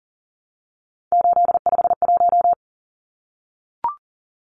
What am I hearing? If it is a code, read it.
Morse code, heard as a keyed tone, about 38 wpm, 700 Hz. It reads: 851
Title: Numbers - 3-Digit 40wpm